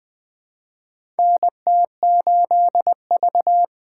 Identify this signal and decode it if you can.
Morse code, heard as a keyed tone, about 20 wpm, 700 Hz. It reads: NT8V
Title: USA Call Signs - 4 Characters - 2x Character Spacing - 20wpm